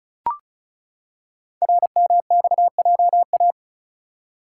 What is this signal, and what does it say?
Morse code, heard as a keyed tone, about 36 wpm, 700 Hz. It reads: RMXJA
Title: Code Groups - 5 Letters - 35wpm